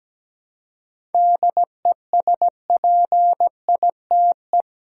Morse code, heard as a keyed tone, about 17 wpm, 700 Hz. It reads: DESPITE
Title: Seven Letter Words - 17wpm